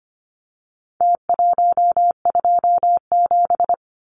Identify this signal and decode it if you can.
Morse code, heard as a keyed tone, about 25 wpm, 700 Hz. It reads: T127
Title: Numbers - Contest Style - 4-Digit 25wpm